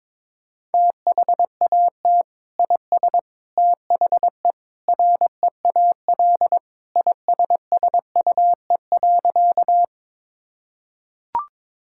Morse code, heard as a keyed tone, about 22 wpm, 700 Hz. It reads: THAT IS THE REAL ISSUE.
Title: Sentences from Top 1000 Words - Review Part 6 - 22wpm